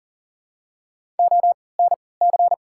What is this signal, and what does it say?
Morse code, heard as a keyed tone, about 38 wpm, spaced as for 22 wpm, 700 Hz. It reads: ONC
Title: 3x Character Spacing - Top 300 Words - 40wpm